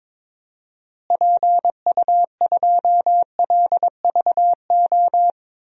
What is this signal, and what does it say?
Morse code, heard as a keyed tone, about 22 wpm, 700 Hz. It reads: PU2LVO
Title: DXCC-100 Call Signs - 6 Characters - Repeated 3x - 22wpm